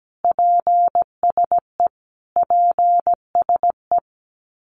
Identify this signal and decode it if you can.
Morse code, heard as a keyed tone, about 17 wpm, 700 Hz. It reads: PSE PSE
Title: Abbreviations Repeated 3x - 17wpm